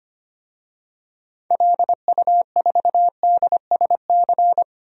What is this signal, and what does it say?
Morse code, heard as a keyed tone, about 25 wpm, 700 Hz. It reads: LU4DSC
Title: DXCC-100 Call Signs - 6 Characters - 25wpm